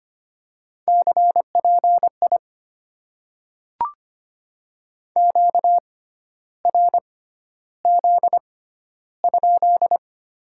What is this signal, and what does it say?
Morse code, heard as a keyed tone, about 25 wpm, spaced as for 12 wpm, 700 Hz. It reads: CPI QRZ?
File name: Top 90 QSO Elements - 6x Character Spacing - 25wpm